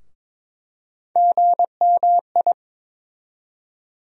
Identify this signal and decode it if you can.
Morse code, heard as a keyed tone, about 22 wpm, 700 Hz. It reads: GMI
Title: USA Call Sign Postfixes - 3 Letters 22wpm